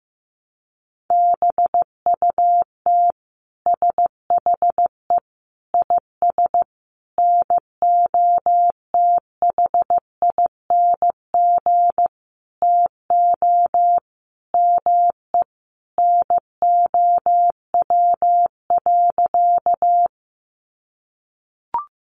Morse code, heard as a keyed tone, about 15 wpm, 700 Hz. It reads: BUT SHE IS NOTHING TO ME NOW.
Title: Sentences from Top 500 Words 15wpm